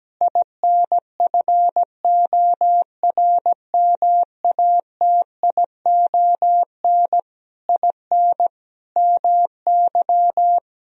Morse code, heard as a keyed tone, about 17 wpm, 700 Hz. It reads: INFORMATION IN MY
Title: Sentences from Top 1000 Words - Review Part 3 - 17wpm